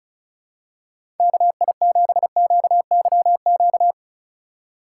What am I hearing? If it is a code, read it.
Morse code, heard as a keyed tone, about 36 wpm, 700 Hz. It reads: KI7QYQ
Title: USA Call Signs - 6 Characters - 4x Character Spacing - 35wpm